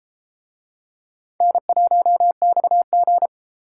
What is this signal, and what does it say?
Morse code, heard as a keyed tone, about 32 wpm, 700 Hz. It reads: N1XG